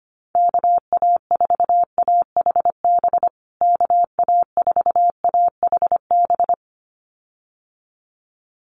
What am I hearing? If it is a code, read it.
Morse code, heard as a keyed tone, about 25 wpm, 700 Hz. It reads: KA4AHB KA4AHB
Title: USA Call Signs Repeated 2x 25wpm